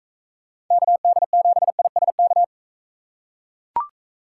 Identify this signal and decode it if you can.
Morse code, heard as a keyed tone, about 39 wpm, 700 Hz. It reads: KD7ISK